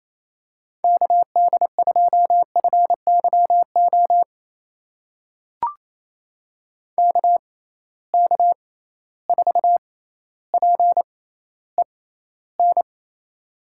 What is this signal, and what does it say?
Morse code, heard as a keyed tone, about 28 wpm, spaced as for 14 wpm, 700 Hz. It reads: KD2FYO KK4PEN
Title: USA Call Signs - 6 Characters - 6x Character Spacing - 28wpm